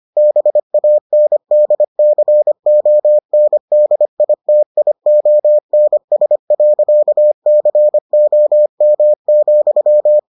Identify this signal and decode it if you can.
Morse code, heard as a keyed tone, about 25 wpm, 600 Hz. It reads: BANDCONDITIONS.COM,